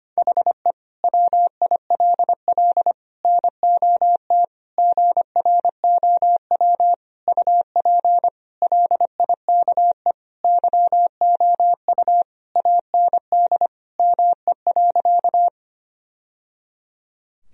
Morse code, heard as a keyed tone, about 25 wpm, 700 Hz. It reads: HE WILL NOT GROW UP LIKE YOU AND ME.
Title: Sentences from Top 1000 Words - Review Part 1 - 25wpm